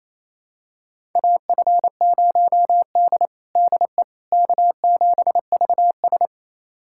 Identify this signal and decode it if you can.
Morse code, heard as a keyed tone, about 28 wpm, 700 Hz. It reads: AF0D DE K7VS
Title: USA Callsign DE Callsign - 4 Characters - 28wpm